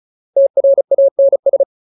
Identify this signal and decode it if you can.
Morse code, heard as a keyed tone, about 34 wpm, 550 Hz. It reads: TRANS
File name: Ultimate Word List / 35WPM Ultimate